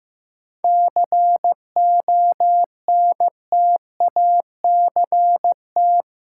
Morse code, heard as a keyed tone, about 15 wpm, 700 Hz. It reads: CONTACT
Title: Sentences from Top 1000 Words - Review Part 7 - New Word Drill - 15wpm